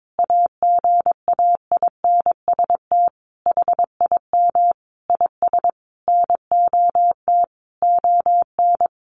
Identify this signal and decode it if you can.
Morse code, heard as a keyed tone, about 22 wpm, 700 Hz. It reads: AGAINST HIM IS NOT ON
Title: Sentences from Top 1400 Words 22wpm